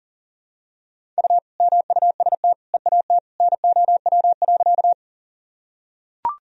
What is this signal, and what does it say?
Morse code, heard as a keyed tone, about 37 wpm, 700 Hz. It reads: U MUST EAT NOW.
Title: Sentences from Top 300 Words and Abbreviations 40wpm